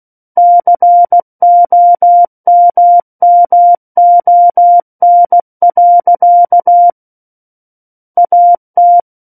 Morse code, heard as a keyed tone, about 16 wpm, 700 Hz. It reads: COMMON. AT
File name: Book One / 07-Chapter 7 - 16 WPM